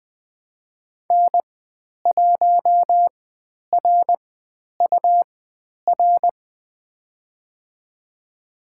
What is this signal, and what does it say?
Morse code, heard as a keyed tone, about 20 wpm, spaced as for 10 wpm, 700 Hz. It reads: N1RUR
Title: 102 Lesson 17g - Letter u - Call Signs 20wpm